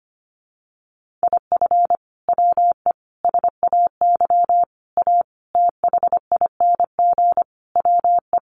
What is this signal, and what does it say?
Morse code, heard as a keyed tone, about 25 wpm, 700 Hz. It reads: IF WE SAY A THING WE
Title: Sentences from Top 1200 Words 25wpm